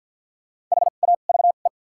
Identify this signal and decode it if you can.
Morse code, heard as a keyed tone, about 53 wpm, spaced as for 37 wpm, 700 Hz. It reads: HAVE
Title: 2x Character Spacing - Top 100 Words - 50wpm